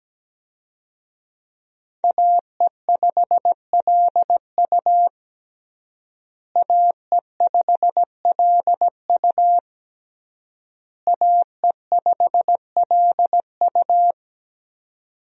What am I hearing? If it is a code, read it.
Morse code, heard as a keyed tone, about 17 wpm, 700 Hz. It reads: AE5LU AE5LU AE5LU
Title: USA Call Signs - 5 Characters - Repeated 3x - 17wpm